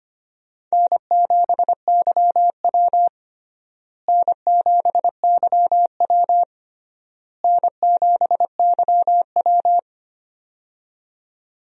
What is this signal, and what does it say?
Morse code, heard as a keyed tone, about 25 wpm, 700 Hz. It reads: N7YW N7YW N7YW
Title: USA Call Signs - 4 Characters - Repeated 3x - 25wpm